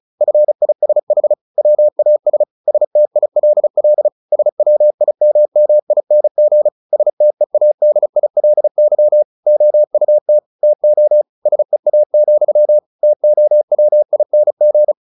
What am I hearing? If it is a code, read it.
Morse code, heard as a keyed tone, about 34 wpm, 600 Hz. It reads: FISH WAS STILL SWIMMING STEADILY OUT TO SEA, TOWING
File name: the old man and the sea / the old man and the sea 1 35wpm